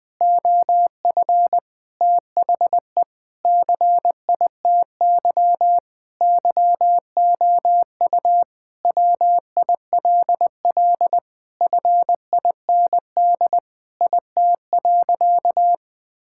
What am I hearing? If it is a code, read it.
Morse code, heard as a keyed tone, about 20 wpm, 700 Hz. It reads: OF THE CITY YOU WILL FIND IT.
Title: Sentences with 4-Word Phrases - 20wpm